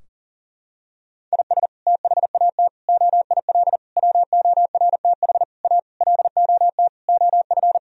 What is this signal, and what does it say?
Morse code, heard as a keyed tone, about 37 wpm, 700 Hz. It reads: IS THAT OIL WORTH A LOT OF
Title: Sentences from Top 1000 Words - Review Part 4 - 40wpm